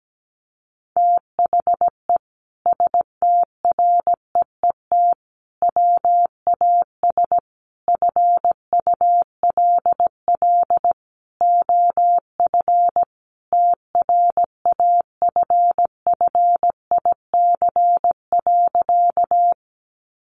Morse code, heard as a keyed tone, about 17 wpm, 700 Hz. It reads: THE STREET WAS FULL OF TRAFFIC.
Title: Sentences from Top 1300 Words 17wpm